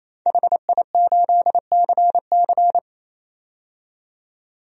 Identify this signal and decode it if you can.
Morse code, heard as a keyed tone, about 27 wpm, 700 Hz. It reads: HI8CC
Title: DXCC-100 Call Signs - 5 Characters - 28wpm